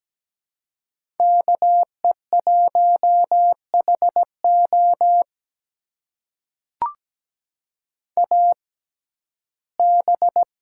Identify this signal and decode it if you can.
Morse code, heard as a keyed tone, about 17 wpm, spaced as for 12 wpm, 700 Hz. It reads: KE1HO AB
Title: USA Call Signs - 5 Characters - 6x Character Spacing - 17wpm